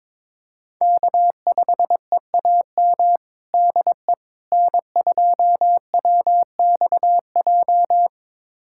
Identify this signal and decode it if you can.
Morse code, heard as a keyed tone, about 22 wpm, 700 Hz. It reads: K5EAM DE N2WXJ
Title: USA Callsign DE Callsign - 5 Characters - 22wpm